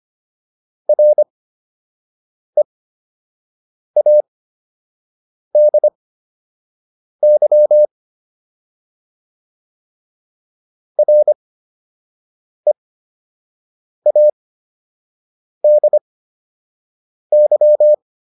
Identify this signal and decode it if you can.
Morse code, heard as a keyed tone, about 25 wpm, spaced as for 6 wpm, 600 Hz. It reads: READY READY